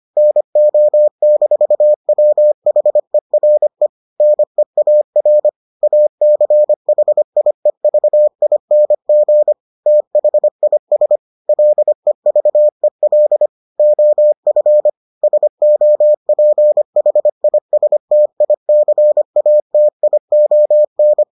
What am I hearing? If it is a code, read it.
Morse code, heard as a keyed tone, about 25 wpm, 600 Hz. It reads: NO-WHERE NEAR ACHIEVING THIS LEVEL OF SOPHISTICATION